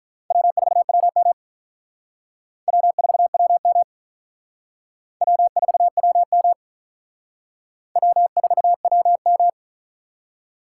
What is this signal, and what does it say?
Morse code, heard as a keyed tone, about 45 wpm, 700 Hz. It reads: W4WM W4WM W4WM W4WM